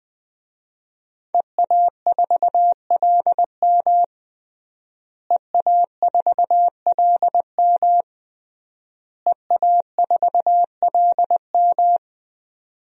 Morse code, heard as a keyed tone, about 20 wpm, 700 Hz. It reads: EA4LM EA4LM EA4LM
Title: DXCC-100 Call Signs - 5 Characters - Repeated 3x - 20wpm